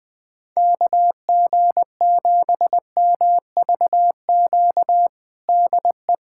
Morse code, heard as a keyed tone, about 20 wpm, 700 Hz. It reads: KG7MVQ DE
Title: USA Callsign DE Callsign 20wpm